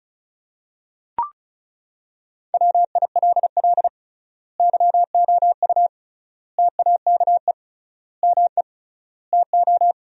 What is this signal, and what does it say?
Morse code, heard as a keyed tone, about 34 wpm, 700 Hz. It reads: WILL YOU TAKE ME TO
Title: 3x Word-Spacing - Sentences from Top 200 Words - 35wpm